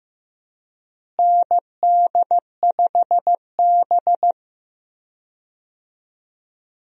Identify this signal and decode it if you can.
Morse code, heard as a keyed tone, about 15 wpm, 700 Hz. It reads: ND5B
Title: USA Call Signs - 4 Characters - Repeated 3x - 15wpm